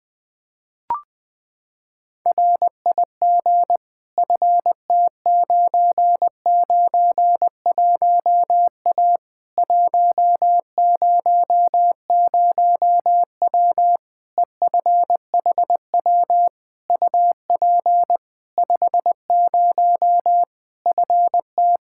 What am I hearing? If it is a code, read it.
Morse code, heard as a keyed tone, about 20 wpm, 700 Hz. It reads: RIG FT991A 100W EFHW UP 50 FT
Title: Practice QSO - Exchange 2 - Info - 20wpm